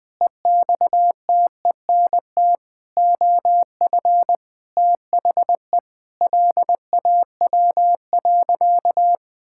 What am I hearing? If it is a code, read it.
Morse code, heard as a keyed tone, about 20 wpm, 700 Hz. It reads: EXTENT OF THE LAW.
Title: Sentences with 7-Word Phrases - 20wpm